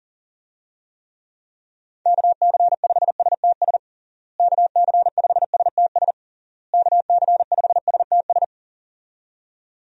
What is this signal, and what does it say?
Morse code, heard as a keyed tone, about 37 wpm, 700 Hz. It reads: KC5STS KC5STS KC5STS
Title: USA Call Signs - 6 Characters - Repeated 3x - 40wpm